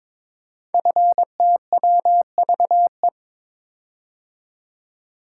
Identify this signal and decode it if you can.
Morse code, heard as a keyed tone, about 22 wpm, 700 Hz. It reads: FTWVE
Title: Code Groups - 5 Letters - 22wpm